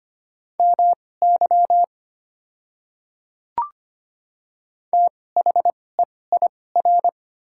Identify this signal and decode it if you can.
Morse code, heard as a keyed tone, about 25 wpm, spaced as for 18 wpm, 690 Hz. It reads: MY THEIR
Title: 2x Character Spacing - Top 100 Words - 25wpm